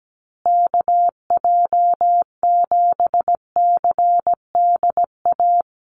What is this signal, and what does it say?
Morse code, heard as a keyed tone, about 17 wpm, 700 Hz. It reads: KJ7CDA
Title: USA Call Signs Repeated 2x 17wpm